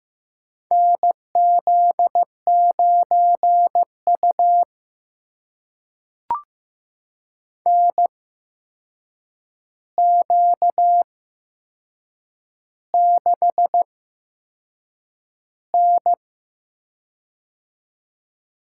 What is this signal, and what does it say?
Morse code, heard as a keyed tone, about 15 wpm, spaced as for 6 wpm, 700 Hz. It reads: NZ9U NQ6N
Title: USA Call Signs - 8x Character Spacing - 15wpm